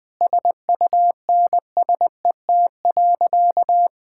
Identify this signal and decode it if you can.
Morse code, heard as a keyed tone, about 21 wpm, 700 Hz. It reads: SUNSET.
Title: Sentences with 7-Word Phrases - 20wpm